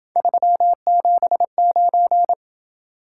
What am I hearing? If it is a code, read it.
Morse code, heard as a keyed tone, about 26 wpm, 700 Hz. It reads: 379